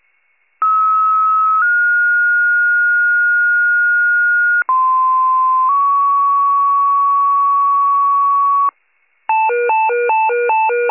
Two-tone page, heard as a one-second tone, then a three-second tone, about 1290 then 1430 Hz, about 1040 then 1120 Hz, with noise. It is heard from 0.6 to 8.7 s.
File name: pager / 95 RESCUE / wcalert